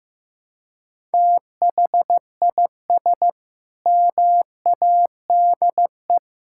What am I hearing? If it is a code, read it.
Morse code, heard as a keyed tone, about 16 wpm, 700 Hz. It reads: THIS MADE
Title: Sentences from Top 1000 Words - Review Part 6 - 15wpm